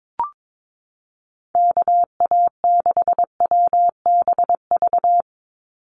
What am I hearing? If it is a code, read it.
Morse code, heard as a keyed tone, about 22 wpm, 700 Hz. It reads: KA6WBV
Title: USA Call Signs 22wpm